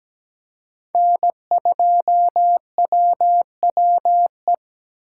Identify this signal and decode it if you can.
Morse code, heard as a keyed tone, about 17 wpm, 700 Hz. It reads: N2WWE
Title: USA Call Signs - 5 Characters - 3x Character Spacing - 17wpm